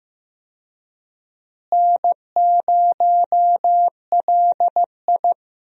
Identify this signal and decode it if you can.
Morse code, heard as a keyed tone, about 15 wpm, 700 Hz. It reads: N0LI
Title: USA Call Signs - 4 Characters - Repeated 3x - 15wpm